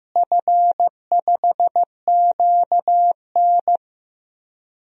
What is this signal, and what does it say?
Morse code, heard as a keyed tone, about 15 wpm, 700 Hz. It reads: F5QN
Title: DXCC-100 Call Signs - 4 Characters - Repeated 3x - 15wpm